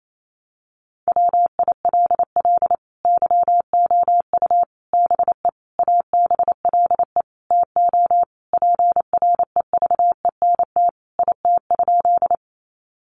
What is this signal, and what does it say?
Morse code, heard as a keyed tone, about 28 wpm, 700 Hz. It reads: WILL YOU BE ABLE TO PREVENT IT?
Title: Sentences from Top 1100 Words 28wpm